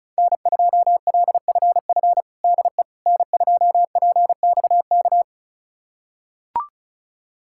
Morse code, heard as a keyed tone, about 36 wpm, 700 Hz. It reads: N2LFF DE N2PXK
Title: USA Callsign DE Callsign - 5 Characters - 35wpm